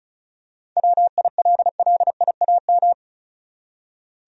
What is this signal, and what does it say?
Morse code, heard as a keyed tone, about 34 wpm, 700 Hz. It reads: WILLIAM
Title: USA Common Names - 35wpm